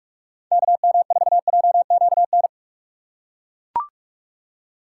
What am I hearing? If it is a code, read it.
Morse code, heard as a keyed tone, about 40 wpm, 700 Hz. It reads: KM4JQN